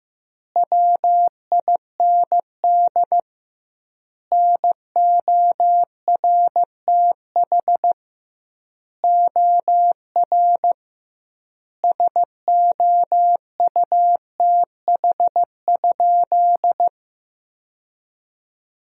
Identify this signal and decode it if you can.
Morse code, heard as a keyed tone, about 15 wpm, 700 Hz. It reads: WIND NORTH OR SOUTH?
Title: 2x Word-Spacing - Sentences from Top 400 Words - 15wpm